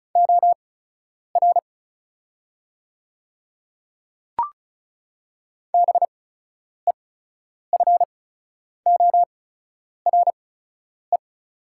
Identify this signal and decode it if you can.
Morse code, heard as a keyed tone, about 36 wpm, spaced as for 10 wpm, 700 Hz. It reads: OR BEFORE
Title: 8x Character Spacing - Top 200 Words - 35wpm